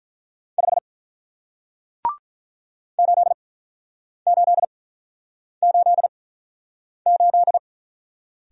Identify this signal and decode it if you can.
Morse code, heard as a keyed tone, about 48 wpm, spaced as for 9 wpm, 700 Hz. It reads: 5 8888